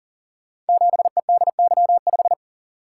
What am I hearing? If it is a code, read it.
Morse code, heard as a keyed tone, about 38 wpm, 690 Hz. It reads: 7EDY5
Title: Code Groups - 5 Letter:Numbers - 40wpm